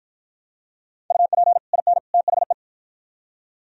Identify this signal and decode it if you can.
Morse code, heard as a keyed tone, about 52 wpm, 700 Hz. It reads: UP IN THE